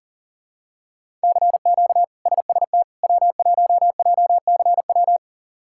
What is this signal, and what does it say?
Morse code, heard as a keyed tone, about 40 wpm, 700 Hz. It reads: CQ SST W1JCW
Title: K1USN SST QSO Practice - 40wpm